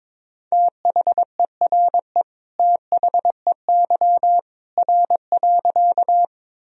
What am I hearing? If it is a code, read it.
Morse code, heard as a keyed tone, about 22 wpm, 700 Hz. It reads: THERE THEY R.
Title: Sentences from Top 100 Words and Abbreviations 22wpm